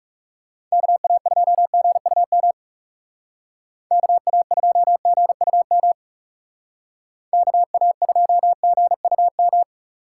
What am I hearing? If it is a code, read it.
Morse code, heard as a keyed tone, about 35 wpm, 700 Hz. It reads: KA2GUM KA2GUM KA2GUM
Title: Speed Racing - USA 2x3 Call Signs 30wpm